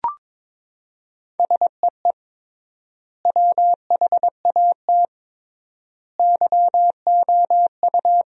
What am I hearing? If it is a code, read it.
Morse code, heard as a keyed tone, about 22 wpm, 700 Hz. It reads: SEE WHAT YOU
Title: 3x Word-Spacing - Sentences from Top 500 Words - 22wpm